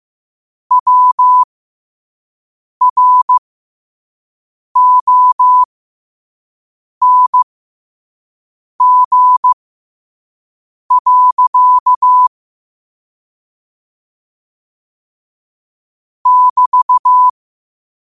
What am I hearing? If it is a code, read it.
Morse code, heard as a keyed tone, about 15 wpm, spaced as for 5 wpm, 1000 Hz. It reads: WRONG. <BT>
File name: qotdfiles / 5wpm